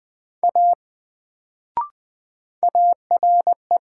Morse code, heard as a keyed tone, about 20 wpm, 700 Hz. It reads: A ARE
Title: Sentences from Top 100 Words - New Word Drill - 20wpm